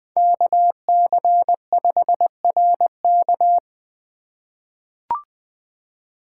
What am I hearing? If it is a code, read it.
Morse code, heard as a keyed tone, about 20 wpm, 700 Hz. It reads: KC5RK
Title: USA Call Signs - 5 Characters - 3x Character Spacing - 20wpm